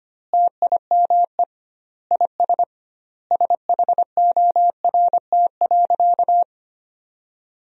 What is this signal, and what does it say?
Morse code, heard as a keyed tone, about 25 wpm, 700 Hz. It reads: TIME IS SHORT.
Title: 2x Word-Spacing - Sentences from Top 500 Words - 25wpm